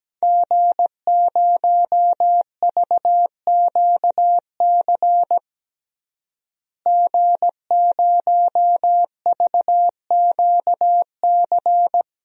Morse code, heard as a keyed tone, about 17 wpm, 700 Hz. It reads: G0VQC G0VQC
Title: DXCC-100 Call Signs - 5 Characters - Repeated 2x - 17wpm